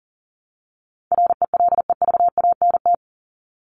Morse code, heard as a keyed tone, about 40 wpm, 700 Hz. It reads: RELEVANT
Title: Top 1001-1250 Words 40wpm